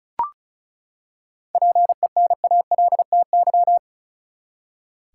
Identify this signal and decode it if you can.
Morse code, heard as a keyed tone, about 36 wpm, 700 Hz. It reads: PENALTY
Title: Seven Letter Words - 35wpm